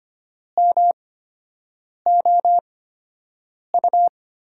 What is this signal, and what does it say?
Morse code, heard as a keyed tone, about 25 wpm, spaced as for 7 wpm, 700 Hz. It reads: MOU
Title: USA Call Signs - 6 Characters - 8x Character Spacing - 25wpm